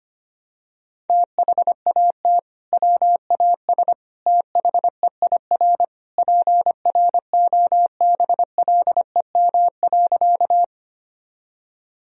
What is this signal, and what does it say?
Morse code, heard as a keyed tone, about 25 wpm, 700 Hz. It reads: THAT WAS THEIR PROBLEM.
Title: Sentences from Top 1000 Words - Review Part 2 - 25wpm